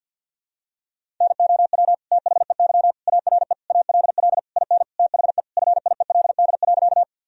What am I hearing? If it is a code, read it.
Morse code, heard as a keyed tone, about 50 wpm, 700 Hz. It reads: NOW THEY ARE ALL IN THE FIELD.
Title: Sentences from Top 1000 Words - Review Part 2 - 50wpm